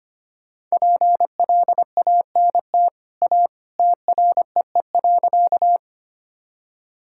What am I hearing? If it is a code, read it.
Morse code, heard as a keyed tone, about 25 wpm, 700 Hz. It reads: PLANT A TREE.
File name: Easy Sentences with Hints - 25wpm